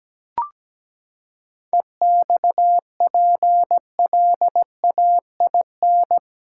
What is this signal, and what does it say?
Morse code, heard as a keyed tone, about 17 wpm, 700 Hz. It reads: EXPLAIN
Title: Seven Letter Words - 17wpm